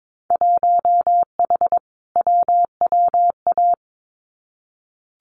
Morse code, heard as a keyed tone, about 22 wpm, 700 Hz. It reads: 1H WWA
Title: Winter Field Day Exchanges 22wpm